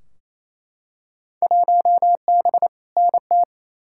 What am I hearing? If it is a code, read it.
Morse code, heard as a keyed tone, about 28 wpm, 700 Hz. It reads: 1B NT
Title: Field Day Exchanges Encore - 28wpm